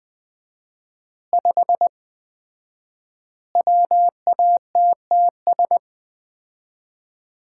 Morse code, heard as a keyed tone, about 20 wpm, 700 Hz. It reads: 5 WATTS
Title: Common QSO Phrases - 4x Word-Spacing - 20wpm